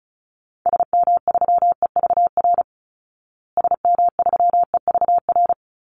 Morse code, heard as a keyed tone, about 34 wpm, 700 Hz. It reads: SM3EVR SM3EVR
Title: Call Signs Repeated 2x - 35wpm